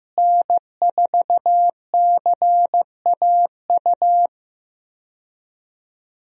Morse code, heard as a keyed tone, about 15 wpm, 700 Hz. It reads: N4CAU
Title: USA Call Signs - 5 Characters - Repeated 2x - 15wpm